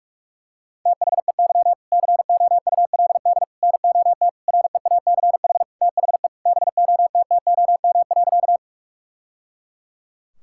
Morse code, heard as a keyed tone, about 46 wpm, 700 Hz. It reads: THEY COULD NOT REACH THE BOTTOM.
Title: Sentences from Top 1000 Words - Review Part 6 - 45wpm